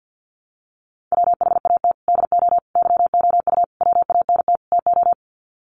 Morse code, heard as a keyed tone, about 50 wpm, 700 Hz. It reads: WHAT DO YOU WANT TO
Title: Sets of 5 Words 50wpm